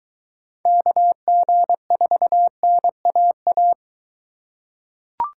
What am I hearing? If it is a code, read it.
Morse code, heard as a keyed tone, about 22 wpm, 700 Hz. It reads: KG4NAA